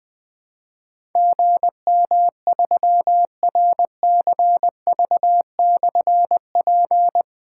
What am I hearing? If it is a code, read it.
Morse code, heard as a keyed tone, about 20 wpm, 700 Hz. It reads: GM3RCV/P
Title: Difficult Call Signs 20wpm